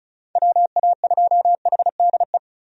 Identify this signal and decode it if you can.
Morse code, heard as a keyed tone, about 34 wpm, 700 Hz. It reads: WA2HDE
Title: USA Call Signs - 6 Characters - 35wpm